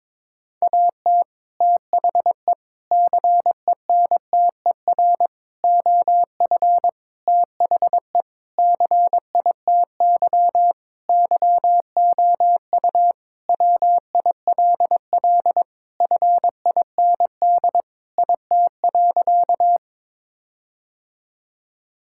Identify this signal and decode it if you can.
Morse code, heard as a keyed tone, about 22 wpm, 700 Hz. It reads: AT THE CENTER OF THE CITY YOU WILL FIND IT.
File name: practice / 22wpm / Sentences with 4-Word Phrases - 22wpm